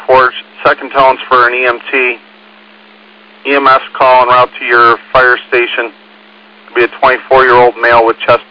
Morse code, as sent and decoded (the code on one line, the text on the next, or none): none